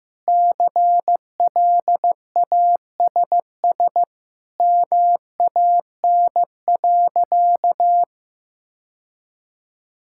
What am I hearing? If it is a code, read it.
Morse code, heard as a keyed tone, about 15 wpm, 700 Hz. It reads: CLASS MAN.
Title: Sentences from Top 1000 Words - Review Part 6 - 15wpm